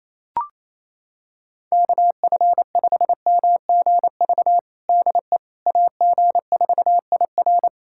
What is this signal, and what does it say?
Morse code, heard as a keyed tone, about 28 wpm, 700 Hz. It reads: KF5MGV DE AG4IR
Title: USA Callsign DE Callsign 28wpm